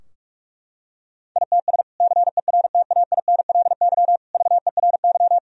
Morse code, heard as a keyed tone, about 46 wpm, 700 Hz. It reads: ITS CERTAINLY VERY
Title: Sentences from Top 1000 Words - Review Part 2 - 45wpm